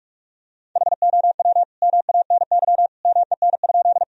Morse code, heard as a keyed tone, about 46 wpm, 700 Hz. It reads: HOW MANY MEN?
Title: Sentences from Top 1000 Words - Review Part 1 - 45wpm